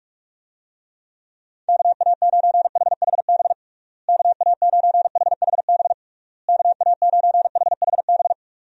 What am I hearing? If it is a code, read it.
Morse code, heard as a keyed tone, about 41 wpm, 700 Hz. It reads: KA9HHB KA9HHB KA9HHB
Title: USA Call Signs - 6 Characters - Repeated 3x - 45wpm